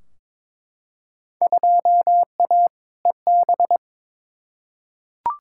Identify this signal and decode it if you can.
Morse code, heard as a keyed tone, about 21 wpm, 700 Hz. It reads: 2A EB
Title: Field Day Exchanges - 22wpm